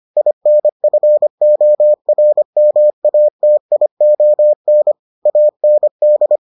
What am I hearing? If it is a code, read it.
Morse code, heard as a keyed tone, about 25 wpm, 600 Hz. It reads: INFORMATION AND